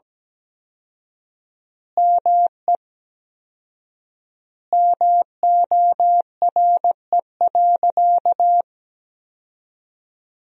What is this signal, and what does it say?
Morse code, heard as a keyed tone, about 17 wpm, 700 Hz. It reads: ME MORE.
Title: 4x Word-Spacing - Sentences from Top 300 Words - 17wpm